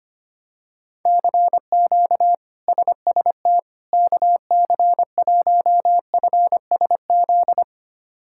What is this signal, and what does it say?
Morse code, heard as a keyed tone, about 25 wpm, 700 Hz. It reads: CQ SST KC1FSZ
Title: K1USN SST QSO Practice - 25wpm